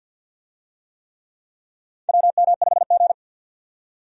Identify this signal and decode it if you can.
Morse code, heard as a keyed tone, about 52 wpm, 700 Hz. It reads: WM5G
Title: USA Call Signs - 4 Characters - 50wpm